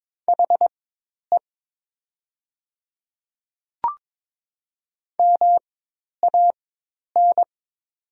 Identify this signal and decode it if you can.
Morse code, heard as a keyed tone, about 22 wpm, spaced as for 10 wpm, 700 Hz. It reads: HE MAN
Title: 4x Character Spacing - Top 200 Words - 22wpm